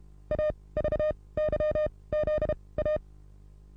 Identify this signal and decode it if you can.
Morse code, heard as a keyed tone, about 31 wpm, spaced as for 21 wpm, 610 Hz. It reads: AVYZA
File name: repos / Morsekurs / DJ4UF Morsen27